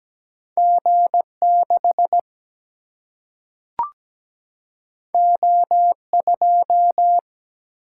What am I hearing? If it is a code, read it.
Morse code, heard as a keyed tone, about 17 wpm, 700 Hz. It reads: G6 O2
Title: Single-Letter Single-Number 17wpm